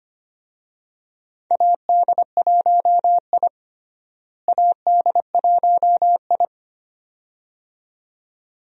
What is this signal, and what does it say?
Morse code, heard as a keyed tone, about 25 wpm, 700 Hz. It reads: AD1I AD1I
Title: USA Call Signs - 4 Characters - Repeated 2x - 25wpm